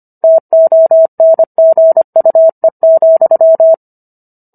Morse code, heard as a keyed tone, about 25 wpm, 650 Hz. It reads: TONGUE,